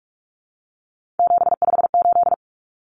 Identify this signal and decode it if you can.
Morse code, heard as a keyed tone, about 44 wpm, 700 Hz. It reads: 758
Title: Numbers - 3-Digit 45wpm